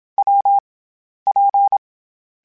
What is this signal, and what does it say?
Morse code, heard as a keyed tone, about 27 wpm, 800 Hz. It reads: W P
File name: Variable Speed / Variable Speed-mp3s / WPJ WPJ1 sp @25